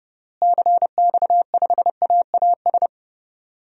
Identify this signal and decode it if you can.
Morse code, heard as a keyed tone, about 32 wpm, 700 Hz. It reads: CX5AAS
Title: DXCC-100 Call Signs - 6 Characters - Repeated 3x - 30wpm